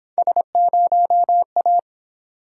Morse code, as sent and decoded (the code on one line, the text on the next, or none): ... ----- .-
S0A